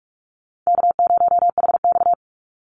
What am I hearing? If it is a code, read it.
Morse code, heard as a keyed tone, about 44 wpm, 700 Hz. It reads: K0HX
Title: Easy Call Signs - 45wpm